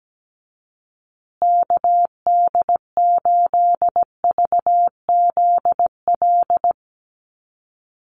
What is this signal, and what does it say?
Morse code, heard as a keyed tone, about 17 wpm, 700 Hz. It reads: KD8VZL
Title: USA Call Signs Repeated 3x 17wpm